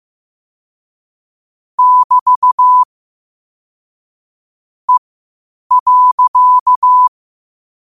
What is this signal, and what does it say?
Morse code, heard as a keyed tone, about 15 wpm, spaced as for 8 wpm, 1000 Hz. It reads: <BT> E.
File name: 8wpm